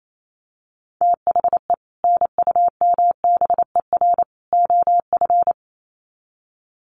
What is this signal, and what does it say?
Morse code, heard as a keyed tone, about 28 wpm, 700 Hz. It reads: THE NUMBER OF
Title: Sets of 3 Words 28wpm